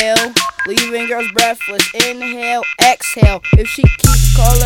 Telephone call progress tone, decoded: off-hook warning tone